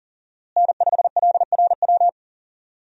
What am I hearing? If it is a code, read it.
Morse code, heard as a keyed tone, about 40 wpm, 700 Hz. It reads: N5LRW
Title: USA Call Signs - 5 Characters - Repeated 2x - 40wpm